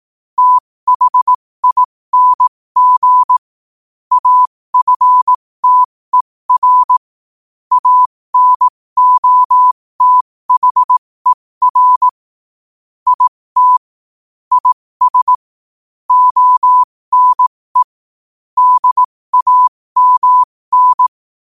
Morse code, heard as a keyed tone, about 18 wpm, spaced as for 15 wpm, 1000 Hz. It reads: THING AFTER ANOTHER IT IS ONE DAMN
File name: qotdfiles / 15wpm